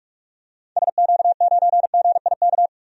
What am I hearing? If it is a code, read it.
Morse code, heard as a keyed tone, about 46 wpm, 700 Hz. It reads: SQ9GIK
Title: DXCC-100 Call Signs - 6 Characters - Repeated 3x - 45wpm